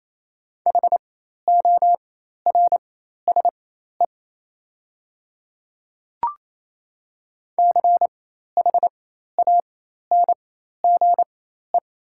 Morse code, heard as a keyed tone, about 28 wpm, spaced as for 13 wpm, 700 Hz. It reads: HORSE CHANGE
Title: 4x Character Spacing - Top 500 Words - 28wpm